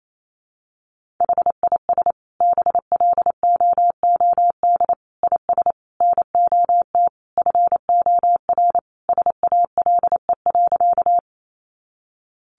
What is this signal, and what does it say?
Morse code, heard as a keyed tone, about 28 wpm, 700 Hz. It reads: HIS BLOOD IS NOT FOR SALE.
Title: Sentences from Top 1400 Words 28wpm